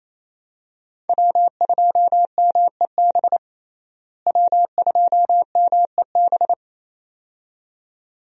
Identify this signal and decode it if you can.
Morse code, heard as a keyed tone, about 28 wpm, 700 Hz. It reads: W2MEB W2MEB
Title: USA Call Signs - 5 Characters - Repeated 2x - 28wpm